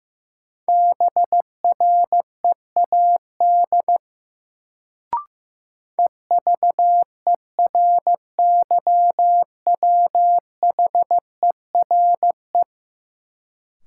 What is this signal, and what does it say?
Morse code, heard as a keyed tone, about 16 wpm, 700 Hz. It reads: BREAD EVERYWHERE
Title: Sentences from Top 1600 Words - New Word Drill Alt - 15wpm